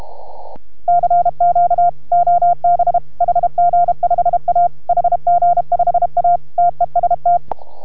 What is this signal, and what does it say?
Morse code, heard as a keyed tone, about 32 wpm, 690 Hz. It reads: CQ OB HG5A HG5A TEST